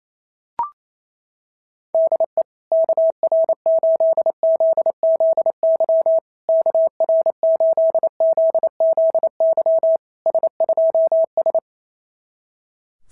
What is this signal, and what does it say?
Morse code, heard as a keyed tone, about 28 wpm, 650 Hz. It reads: DE KR8ZZY KR8ZZY S2S
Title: SOTA QSO Practice 28wpm